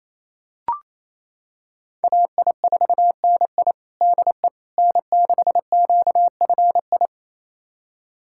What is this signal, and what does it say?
Morse code, heard as a keyed tone, about 28 wpm, 700 Hz. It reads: AI4NI DE N6QFI
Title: USA Callsign DE Callsign - 5 Characters - 28wpm